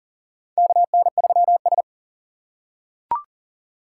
Morse code, heard as a keyed tone, about 40 wpm, 700 Hz. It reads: KN3S
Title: USA Call Signs - 4 Characters - 2x Character Spacing - 40wpm